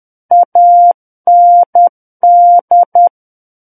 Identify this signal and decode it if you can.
Morse code, heard as a keyed tone, about 10 wpm, 700 Hz. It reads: AND
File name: Book One / 01-Chapter 1 - 10 WPM